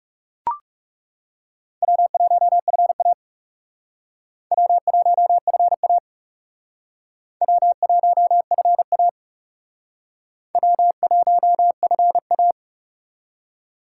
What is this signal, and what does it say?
Morse code, heard as a keyed tone, about 36 wpm, 700 Hz. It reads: W1FA W1FA W1FA W1FA
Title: Speed Racing - USA 1x2 Call Signs 30wpm